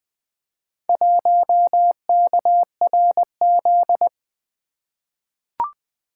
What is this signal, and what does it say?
Morse code, heard as a keyed tone, about 20 wpm, 700 Hz. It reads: 1KRZ
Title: Four Letter-Number Combinations - 20wpm